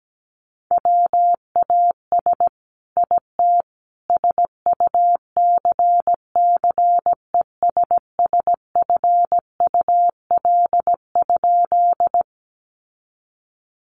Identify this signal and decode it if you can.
Morse code, heard as a keyed tone, about 17 wpm, 700 Hz. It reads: WAS IT SUCCESSFUL?
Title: A Sound Of Thunder 17wpm